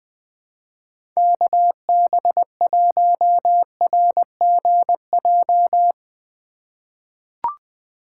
Morse code, heard as a keyed tone, about 20 wpm, 700 Hz. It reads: KB1RGJ
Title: USA Call Signs - 6 Characters - 3x Character Spacing - 20wpm